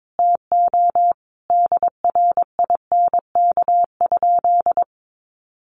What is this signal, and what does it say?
Morse code, heard as a keyed tone, about 22 wpm, 700 Hz. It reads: TO DRINK?
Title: Sentences from Top 1500 Words 22wpm